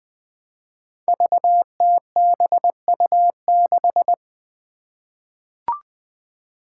Morse code, heard as a keyed tone, about 20 wpm, 700 Hz. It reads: VTBU6
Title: Five Challenging Letter-Number Combinations 20wpm